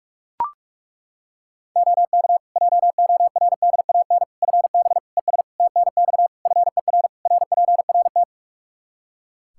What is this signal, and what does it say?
Morse code, heard as a keyed tone, about 41 wpm, 700 Hz. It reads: OK JORDAN FB ES TNX FER RPRT
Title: Practice QSO - Exchange 2 - Info - 45wpm